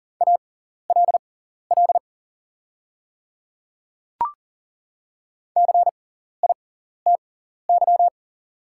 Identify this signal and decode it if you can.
Morse code, heard as a keyed tone, about 39 wpm, spaced as for 14 wpm, 690 Hz. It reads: ALL CITY
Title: 6x Character Spacing - Top 500 Words - 40wpm